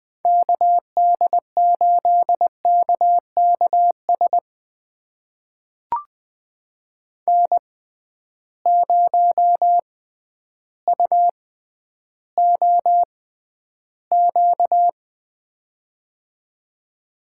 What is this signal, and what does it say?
Morse code, heard as a keyed tone, about 20 wpm, spaced as for 11 wpm, 700 Hz. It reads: KD8KKS N0UOQ
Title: USA Call Signs - 6x Character Spacing - 20wpm